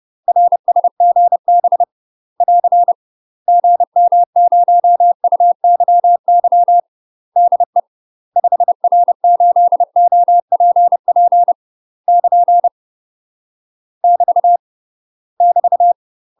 Morse code, heard as a keyed tone, about 30 wpm, 700 Hz. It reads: RSGB <AR> GM0UYY DE 5R8OPP <KN> <BT> <BT>
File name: UK MASTER QSOs TRAINING / 30WPM / UK MASTER QSOs 0050